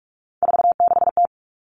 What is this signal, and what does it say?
Morse code, heard as a keyed tone, about 44 wpm, 700 Hz. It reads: 46T
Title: All Cut Numbers - 3-Digit 45wpm